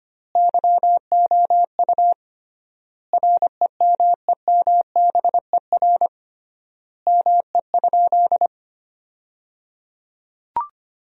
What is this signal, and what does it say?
Morse code, heard as a keyed tone, about 25 wpm, 700 Hz. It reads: YOU REMEMBER ME?
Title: 3x Word-Spacing - Sentences from Top 400 Words - 25wpm